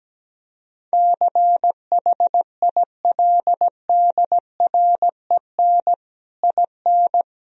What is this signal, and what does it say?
Morse code, heard as a keyed tone, about 17 wpm, 700 Hz. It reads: CHILDREN IN
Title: Sentences from Top 1000 Words - Review Part 4 - 17wpm